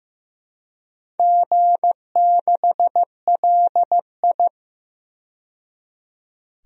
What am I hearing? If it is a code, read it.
Morse code, heard as a keyed tone, about 15 wpm, 700 Hz. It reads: G6LI
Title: DXCC-100 Call Signs - 4 Characters - 15wpm